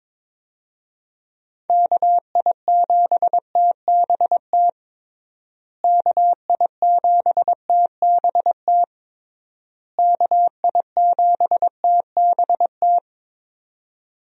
Morse code, heard as a keyed tone, about 22 wpm, 700 Hz. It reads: KI7TBT KI7TBT KI7TBT
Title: USA Call Signs - 6 Characters - Repeated 3x - 22wpm